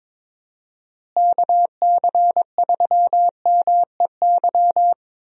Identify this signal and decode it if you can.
Morse code, heard as a keyed tone, about 22 wpm, 700 Hz. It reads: KC3MEY
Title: USA Call Signs - 6 Characters - Repeated 3x - 22wpm